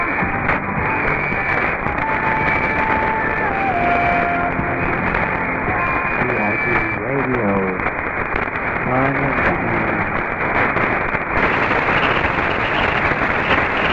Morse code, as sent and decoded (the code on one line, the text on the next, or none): none